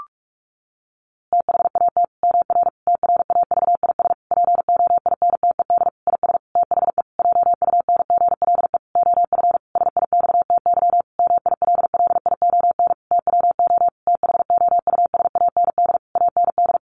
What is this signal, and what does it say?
Morse code, heard as a keyed tone, about 44 wpm, 700 Hz. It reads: THAT MR TRAVIS POINTED IS THE JUNGLE OF SIXTY MILLION TWO THOUSAND AND
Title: A Sound Of Thunder 45wpm